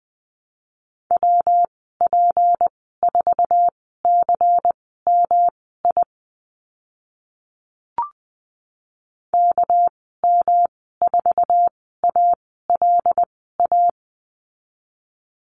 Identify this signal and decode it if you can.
Morse code, heard as a keyed tone, about 20 wpm, spaced as for 14 wpm, 700 Hz. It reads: WP4CMI KM4ALA
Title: 207 Lesson 33g - Letter j - Call Signs 20wpm